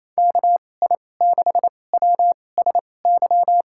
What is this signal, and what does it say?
Morse code, heard as a keyed tone, about 28 wpm, spaced as for 20 wpm, 700 Hz. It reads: KI6WSY
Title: USA Call Signs - 6 Characters - 2x Character Spacing - 28wpm